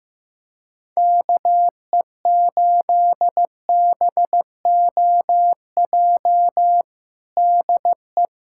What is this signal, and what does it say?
Morse code, heard as a keyed tone, about 15 wpm, 700 Hz. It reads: KE8BOJ DE
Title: USA Callsign DE Callsign - 6 Characters - 15wpm